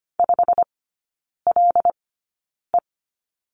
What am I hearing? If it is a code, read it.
Morse code, heard as a keyed tone, about 25 wpm, spaced as for 9 wpm, 700 Hz. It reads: 5LE
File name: Lesson 17e - Letter u - Three Characters 25wpm